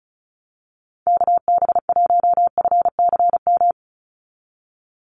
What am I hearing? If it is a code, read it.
Morse code, heard as a keyed tone, about 36 wpm, 700 Hz. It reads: KB1FCM
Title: USA Call Signs 35wpm